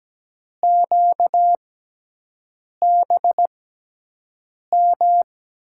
Morse code, heard as a keyed tone, about 17 wpm, spaced as for 6 wpm, 700 Hz. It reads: QBM
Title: USA Call Signs - 5 Characters - 6x Character Spacing - 17wpm